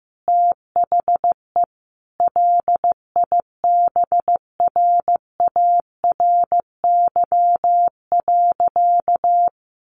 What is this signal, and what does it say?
Morse code, heard as a keyed tone, about 15 wpm, 700 Hz. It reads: THE LIBRARY.
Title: Sentences from Top 1400 Words 15wpm